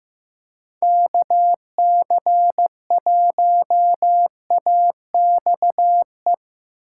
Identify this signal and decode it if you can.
Morse code, heard as a keyed tone, about 15 wpm, 700 Hz. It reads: KC1AXE
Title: USA Call Signs - 2x Character Spacing - 15wpm